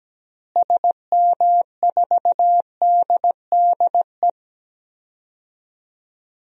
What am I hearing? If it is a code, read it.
Morse code, heard as a keyed tone, about 17 wpm, 700 Hz. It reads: SM4DDE
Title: DXCC-100 Call Signs - 6 Characters - Repeated 2x - 17wpm